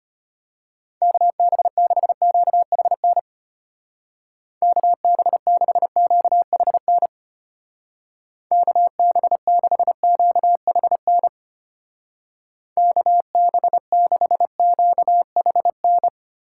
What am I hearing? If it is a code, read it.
Morse code, heard as a keyed tone, about 31 wpm, 700 Hz. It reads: KB6QHN KB6QHN KB6QHN KB6QHN
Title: Speed Racing - USA 2x3 Call Signs 25wpm